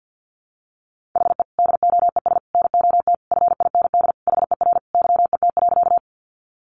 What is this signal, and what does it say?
Morse code, heard as a keyed tone, about 50 wpm, 700 Hz. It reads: HE DOES NOT FIND HER YET.
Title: Easy Phrases 50wpm